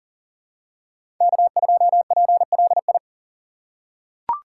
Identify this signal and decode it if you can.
Morse code, heard as a keyed tone, about 40 wpm, 700 Hz. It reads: K2PLI
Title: USA Call Signs - 5 Characters - 3x Character Spacing - 40wpm